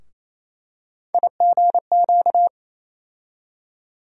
Morse code, heard as a keyed tone, about 28 wpm, 700 Hz. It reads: IGQ
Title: USA Call Sign Postfixes - 3 Letters 28wpm